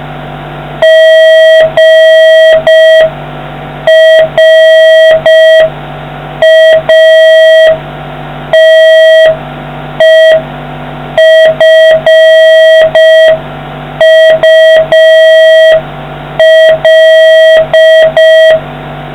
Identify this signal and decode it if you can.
Morse code, heard as a keyed tone, about 5 wpm, 1950 Hz, with noise. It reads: GRATEFUL